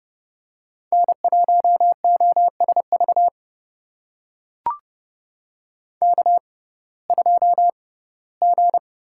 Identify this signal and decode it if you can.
Morse code, heard as a keyed tone, about 30 wpm, spaced as for 18 wpm, 700 Hz. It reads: N1OSV K2G
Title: USA Call Signs - 5 Characters - 6x Character Spacing - 30wpm